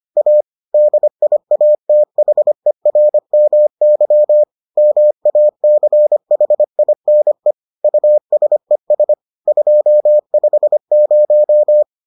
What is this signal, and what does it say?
Morse code, heard as a keyed tone, about 25 wpm, 600 Hz. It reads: A DIATHERMY MACHINE USES 250